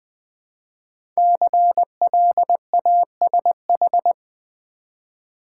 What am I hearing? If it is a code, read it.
Morse code, heard as a keyed tone, about 20 wpm, 700 Hz. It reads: CLASH
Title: 5 Letter Words - First 3 Letters Given - 20wpm